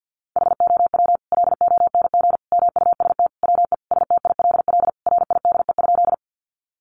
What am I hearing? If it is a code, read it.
Morse code, heard as a keyed tone, about 50 wpm, 700 Hz. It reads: HOW LONG MUST WE STILL RIDE?
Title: Sentences from Top 1200 Words 50wpm